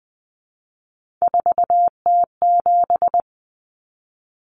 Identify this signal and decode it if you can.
Morse code, heard as a keyed tone, about 20 wpm, 700 Hz. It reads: 4T7
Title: Common Cut Numbers - 3-Digit 20wpm